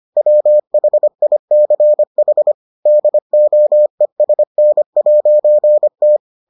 Morse code, heard as a keyed tone, about 25 wpm, 600 Hz. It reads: WHICH DOESN'T